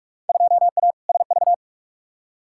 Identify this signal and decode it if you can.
Morse code, heard as a keyed tone, about 44 wpm, 700 Hz. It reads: 2A SV
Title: Field Day Exchanges - 45wpm